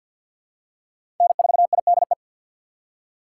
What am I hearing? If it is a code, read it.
Morse code, heard as a keyed tone, about 53 wpm, 700 Hz. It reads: N4IDE
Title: USA Call Signs - 5 Characters - 4x Character Spacing - 50wpm